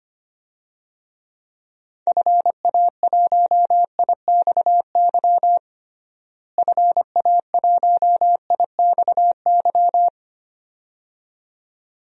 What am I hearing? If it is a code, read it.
Morse code, heard as a keyed tone, about 25 wpm, 700 Hz. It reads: FA1IXY FA1IXY
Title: DXCC-100 Call Signs - 6 Characters - Repeated 2x - 25wpm